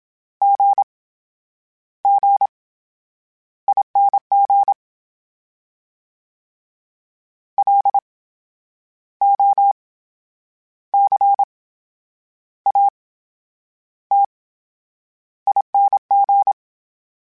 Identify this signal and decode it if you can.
Morse code, heard as a keyed tone, about 27 wpm, spaced as for 9 wpm, 800 Hz. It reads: GGING LOCATING
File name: Special INGs / Special INGs-mp3s / VerySlowToFaster Ing @25